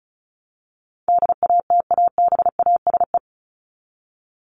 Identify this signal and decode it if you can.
Morse code, heard as a keyed tone, about 34 wpm, 700 Hz. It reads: DATABASE
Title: Top 1751-2000 Words 35wpm